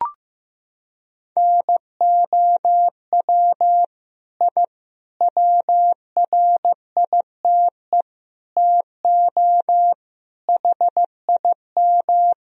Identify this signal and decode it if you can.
Morse code, heard as a keyed tone, about 16 wpm, 700 Hz. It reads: NOW I WRITE TO HIM
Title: Sentences from Top 1000 Words - Review Part 1 - 15wpm